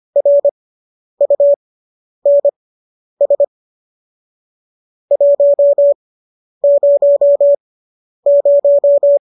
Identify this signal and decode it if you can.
Morse code, heard as a keyed tone, about 25 wpm, spaced as for 10 wpm, 570 Hz. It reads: RUNS 100